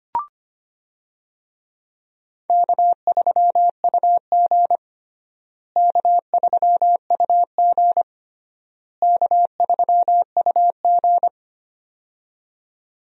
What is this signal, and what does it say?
Morse code, heard as a keyed tone, about 25 wpm, 700 Hz. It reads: K3UG K3UG K3UG
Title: USA Call Signs - 4 Characters - Repeated 3x - 25wpm